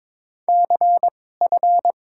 Morse code, heard as a keyed tone, about 21 wpm, spaced as for 16 wpm, 700 Hz. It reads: CF
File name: Top 90 QSO Elements - 2x Character Spacing - 22wpm